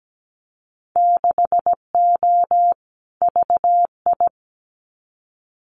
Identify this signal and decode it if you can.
Morse code, heard as a keyed tone, about 17 wpm, 700 Hz. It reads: 6O VI
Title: Winter Field Day Exchanges Encore 17wpm